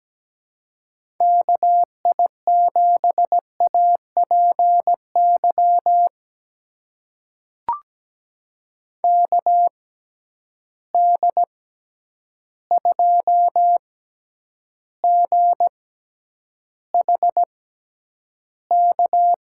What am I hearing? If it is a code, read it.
Morse code, heard as a keyed tone, about 17 wpm, spaced as for 9 wpm, 700 Hz. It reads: KI7APY KD2GHK
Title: USA Call Signs - 6 Characters - 6x Character Spacing - 17wpm